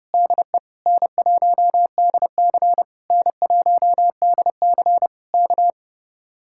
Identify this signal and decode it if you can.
Morse code, heard as a keyed tone, about 30 wpm, 700 Hz. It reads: DE N1DC N1DC K